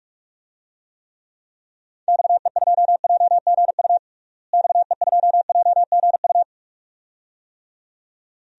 Morse code, heard as a keyed tone, about 44 wpm, 700 Hz. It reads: XE2JGU XE2JGU
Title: DXCC-100 Call Signs - 6 Characters - Repeated 2x - 45wpm